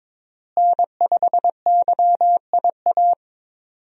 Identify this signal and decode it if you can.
Morse code, heard as a keyed tone, about 21 wpm, 700 Hz. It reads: N5YIA
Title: USA Call Signs - 5 Characters - Repeated 3x - 22wpm